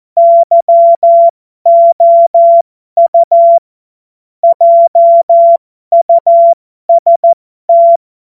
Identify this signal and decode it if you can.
Morse code, heard as a keyed tone, about 14 wpm, spaced as for 12 wpm, 680 Hz. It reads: YOU JUST
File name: Morsekurs / Witze